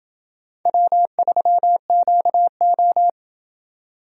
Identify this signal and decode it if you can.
Morse code, heard as a keyed tone, about 29 wpm, 700 Hz. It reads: W3QO